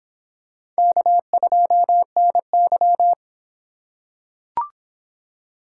Morse code, heard as a keyed tone, about 25 wpm, 700 Hz. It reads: K2NY